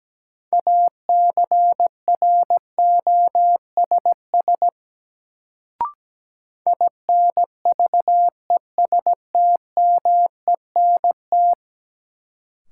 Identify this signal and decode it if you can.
Morse code, heard as a keyed tone, about 17 wpm, 700 Hz. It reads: ACROSS INVESTMENT
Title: Sentences from Top 700 Words - New Word Drill - 17wpm